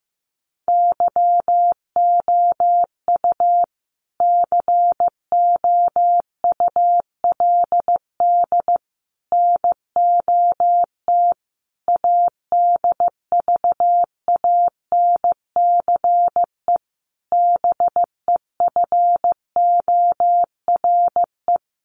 Morse code, heard as a keyed tone, about 15 wpm, 700 Hz. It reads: YOU COULD NOT ADVANCE BEFORE
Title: Sentences from Top 1400 Words 15wpm